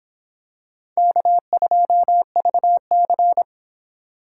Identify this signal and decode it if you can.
Morse code, heard as a keyed tone, about 26 wpm, 700 Hz. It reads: K2VC